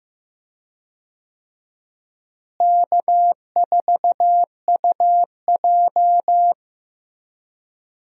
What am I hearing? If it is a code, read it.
Morse code, heard as a keyed tone, about 15 wpm, 700 Hz. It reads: K4UJ
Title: USA Call Signs - 4 Characters - Repeated 3x - 15wpm